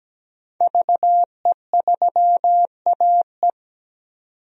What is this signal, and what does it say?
Morse code, heard as a keyed tone, about 17 wpm, 700 Hz. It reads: VE3AE
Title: DXCC-100 Call Signs - 5 Characters - 17wpm